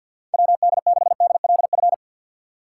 Morse code, heard as a keyed tone, about 53 wpm, 700 Hz. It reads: WD6DLF
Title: USA Call Signs - 6x Character Spacing - 50wpm